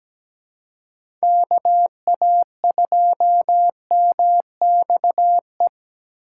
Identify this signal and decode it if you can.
Morse code, heard as a keyed tone, about 17 wpm, 700 Hz. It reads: KA2MXE
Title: USA Call Signs - 6x Character Spacing - 17wpm